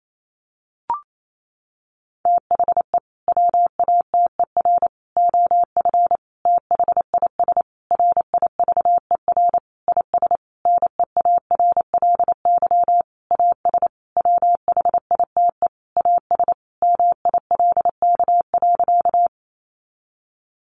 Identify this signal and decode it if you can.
Morse code, heard as a keyed tone, about 28 wpm, 700 Hz. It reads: THE WATER OF THIS RIVER IS NEARLY AS WHITE AS MILK.
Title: Sentences from Top 1500 Words 28wpm